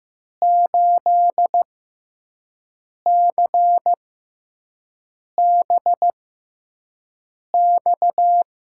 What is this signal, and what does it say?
Morse code, heard as a keyed tone, about 15 wpm, spaced as for 5 wpm, 700 Hz. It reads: 8CBX
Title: USA Call Signs - 5 Characters - 6x Character Spacing - 15wpm